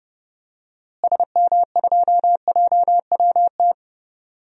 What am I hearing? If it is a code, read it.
Morse code, heard as a keyed tone, about 29 wpm, 700 Hz. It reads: SM2JWT
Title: DXCC-100 Call Signs - 6 Characters - Repeated 2x - 30wpm